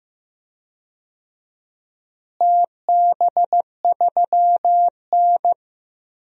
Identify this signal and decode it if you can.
Morse code, heard as a keyed tone, about 15 wpm, 700 Hz. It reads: TB3N
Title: DXCC-100 Call Signs - 4 Characters - Repeated 2x - 15wpm